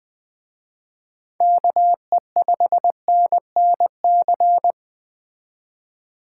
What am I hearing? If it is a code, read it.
Morse code, heard as a keyed tone, about 21 wpm, 700 Hz. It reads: KE5NNC